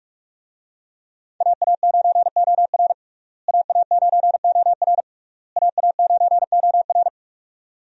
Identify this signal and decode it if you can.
Morse code, heard as a keyed tone, about 46 wpm, 700 Hz. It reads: AA9OR AA9OR AA9OR
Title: USA Call Signs - 5 Characters - Repeated 3x - 45wpm